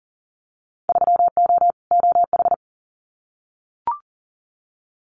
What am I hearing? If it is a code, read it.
Morse code, heard as a keyed tone, about 40 wpm, 700 Hz. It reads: 3O OH
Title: Winter Field Day Exchanges 40wpm